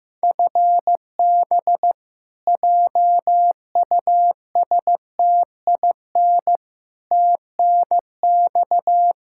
Morse code, heard as a keyed tone, about 15 wpm, 700 Hz. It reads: FB JUSTIN TNX
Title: Practice QSO - Exchange 3 - More Info - 15wpm